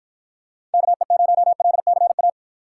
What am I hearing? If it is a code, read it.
Morse code, heard as a keyed tone, about 53 wpm, 700 Hz. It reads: KE0LCA